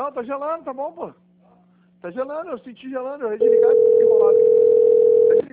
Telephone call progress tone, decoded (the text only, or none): ringback tone